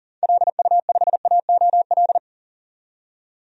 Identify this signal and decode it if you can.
Morse code, heard as a keyed tone, about 37 wpm, 700 Hz. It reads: LU5AOL
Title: DXCC-100 Call Signs - 6 Characters - 40wpm